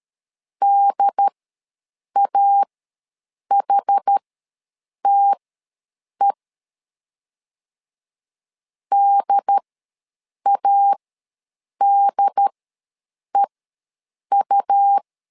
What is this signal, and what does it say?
Morse code, heard as a keyed tone, about 13 wpm, spaced as for 7 wpm, 800 Hz. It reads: DAHTE DADEU
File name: LEZIONI / 2 UHDN / 2 UHD ws 12wpm 1722